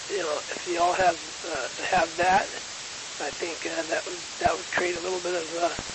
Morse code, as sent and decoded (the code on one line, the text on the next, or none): none